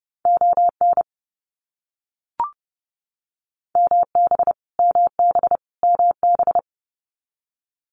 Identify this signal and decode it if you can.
Morse code, heard as a keyed tone, about 30 wpm, 700 Hz. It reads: ON MB MB MB
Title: Canadian Provinces and Territories - Repeated 3x - 30wpm